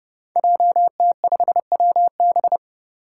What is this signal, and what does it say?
Morse code, heard as a keyed tone, about 30 wpm, 700 Hz. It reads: JT5WB
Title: Five Challenging Letter-Number Combinations 30wpm